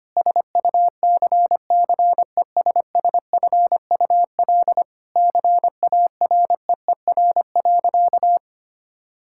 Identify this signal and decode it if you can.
Morse code, heard as a keyed tone, about 25 wpm, 700 Hz. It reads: SUCCESSFUL CAREER.
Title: Sentences from Top 1000 Words - Review Part 6 - 25wpm